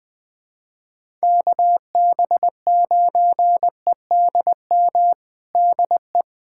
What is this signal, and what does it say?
Morse code, heard as a keyed tone, about 20 wpm, 700 Hz. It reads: KB9EDM DE
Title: USA Callsign DE Callsign 20wpm